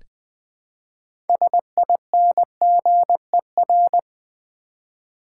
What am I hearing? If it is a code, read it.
Morse code, heard as a keyed tone, about 20 wpm, 700 Hz. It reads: SINGER
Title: Build A Word - 3 Letters At A Time - 6-Letter Words - 20wpm